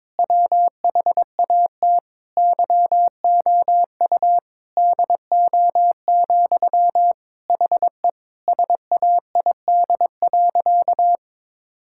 Morse code, heard as a keyed tone, about 21 wpm, 700 Hz. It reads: WHAT YOU DO, HE SAID.
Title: Sentences from Top 1600 Words 22wpm